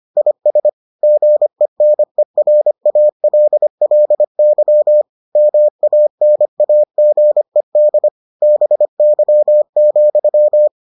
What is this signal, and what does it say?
Morse code, heard as a keyed tone, about 25 wpm, 600 Hz. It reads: IS GENERALLY MANAGED BY,